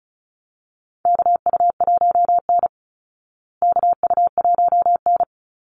Morse code, heard as a keyed tone, about 34 wpm, 700 Hz. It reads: KU1N KU1N
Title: Call Signs Repeated 3x - 35wpm